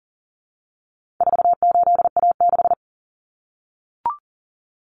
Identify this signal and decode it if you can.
Morse code, heard as a keyed tone, about 40 wpm, 700 Hz. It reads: 48A6
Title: Common Cut Numbers - 4-Digit 40wpm